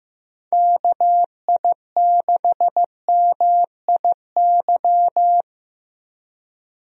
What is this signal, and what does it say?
Morse code, heard as a keyed tone, about 15 wpm, 700 Hz. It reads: KI6MIY
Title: USA Call Signs - 6 Characters - Repeated 2x - 15wpm